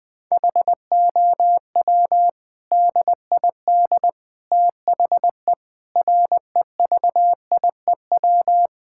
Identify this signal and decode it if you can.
Morse code, heard as a keyed tone, about 20 wpm, 700 Hz. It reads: HOW DID THE REVIEW
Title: Sentences from Top 1000 Words - Review Part 6 - 20wpm